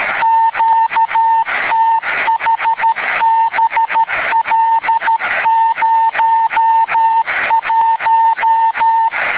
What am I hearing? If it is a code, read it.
Morse code, heard as a keyed tone, about 13 wpm, 900 Hz. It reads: QTH BL01